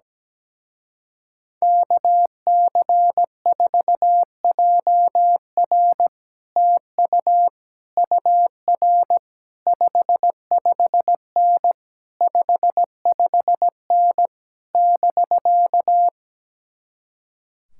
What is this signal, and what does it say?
Morse code, heard as a keyed tone, about 17 wpm, 700 Hz. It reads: KC4JR TU UR 55N 55N <BK>
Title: Speed-Racing Common QSO Phrases - 17wpm